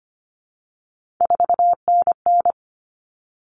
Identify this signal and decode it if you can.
Morse code, heard as a keyed tone, about 25 wpm, 700 Hz. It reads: 4NN
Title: All Cut Numbers - 3-Digit 25wpm